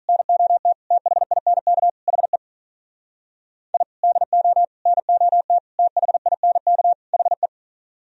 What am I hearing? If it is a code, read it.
Morse code, heard as a keyed tone, about 42 wpm, 700 Hz. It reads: NOT THINK HE I DO NOT THINK HE
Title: Speed-Racing - Sets of 5 Words 35wpm